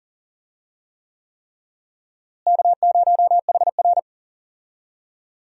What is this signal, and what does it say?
Morse code, heard as a keyed tone, about 40 wpm, 700 Hz. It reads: K0HR
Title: USA Call Signs - 4 Characters - Repeated 3x - 40wpm